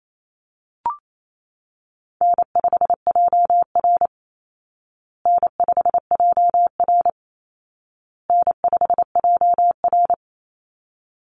Morse code, heard as a keyed tone, about 28 wpm, 700 Hz. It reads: N5JR N5JR N5JR
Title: Easy Call Signs Repeated 3x - 28wpm